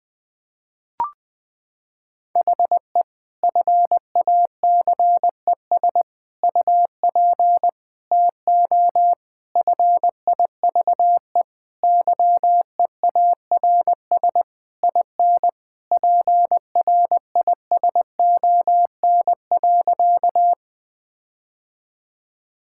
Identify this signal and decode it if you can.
Morse code, heard as a keyed tone, about 21 wpm, 700 Hz. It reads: HE FACES UP TO FIVE YEARS IN PRISON.
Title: Sentences with 6-Word Phrases - 20wpm